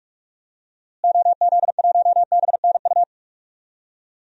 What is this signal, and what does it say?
Morse code, heard as a keyed tone, about 46 wpm, 700 Hz. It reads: OZ1BNU
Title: DXCC-100 Call Signs - 6 Characters - Repeated 2x - 45wpm